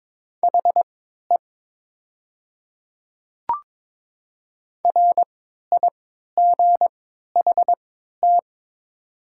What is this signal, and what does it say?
Morse code, heard as a keyed tone, about 22 wpm, spaced as for 13 wpm, 700 Hz. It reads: HE RIGHT
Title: 3x Character Spacing - Top 300 Words - 22wpm